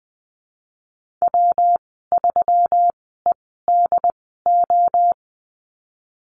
Morse code, heard as a keyed tone, about 19 wpm, spaced as for 14 wpm, 700 Hz. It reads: W3EDO
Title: 186 Lesson 30g - Letter b - Call Signs 20wpm